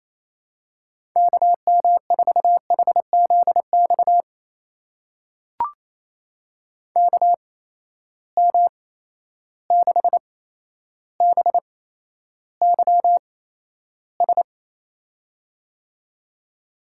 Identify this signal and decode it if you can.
Morse code, heard as a keyed tone, about 28 wpm, spaced as for 12 wpm, 700 Hz. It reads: KM4HZX KM6BYS
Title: USA Call Signs - 6 Characters - 8x Character Spacing - 28wpm